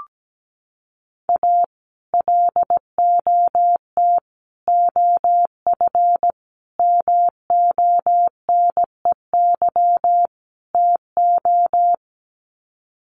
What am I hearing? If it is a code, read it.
Morse code, heard as a keyed tone, about 17 wpm, 700 Hz. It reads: A LOT OF MONEY TO
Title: Sets of 5 Words Encore2 17wpm